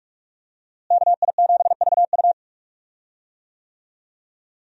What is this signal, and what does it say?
Morse code, heard as a keyed tone, about 41 wpm, 700 Hz. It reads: KI7VU
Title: USA Call Signs - 5 Characters - Repeated 2x - 45wpm